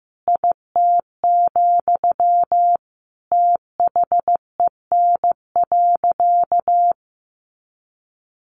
Morse code, heard as a keyed tone, about 15 wpm, 700 Hz. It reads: IT, THEN.
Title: Sentences from Top 1000 Words 15wpm